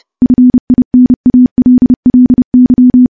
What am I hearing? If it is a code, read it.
Morse code, heard as a keyed tone, about 32 wpm, 250 Hz. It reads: FINALLY
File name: morse-training-30WPM